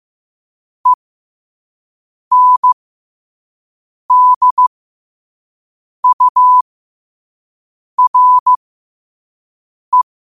Morse code, heard as a keyed tone, about 15 wpm, spaced as for 5 wpm, 1000 Hz. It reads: ENDURE
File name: qotdfiles / 5wpm